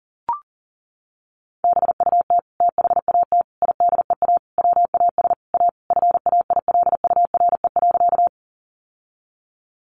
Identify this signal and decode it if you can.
Morse code, heard as a keyed tone, about 40 wpm, 700 Hz. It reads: BUT THAT IDEA WAS A FAILURE.
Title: Sentences from Top 900 Words 40wpm